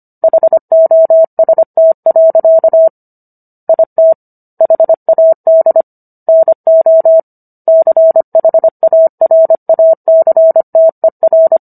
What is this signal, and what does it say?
Morse code, heard as a keyed tone, about 25 wpm, 650 Hz. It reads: HOST. IT HAD NO CHARACTER